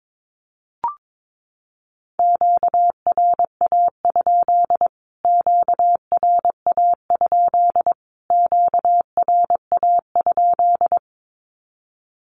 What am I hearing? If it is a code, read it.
Morse code, heard as a keyed tone, about 22 wpm, 700 Hz. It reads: QRA? QRA? QRA?
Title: Q-Codes Repeated 3x 22wpm